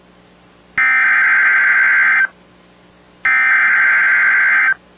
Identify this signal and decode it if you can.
Emergency Alert System (SAME) tones (header bursts), heard as harsh buzzing bursts. It is heard from 0.8 to 4.7 s.